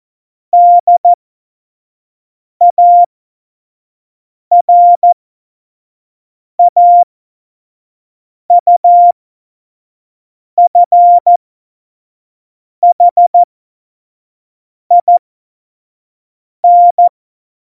Morse code, heard as a keyed tone, about 14 wpm, spaced as for 5 wpm, 700 Hz. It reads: DARAUFHIN
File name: Morsekurs / Witze / Witz 11. Juni 2019, 25 F70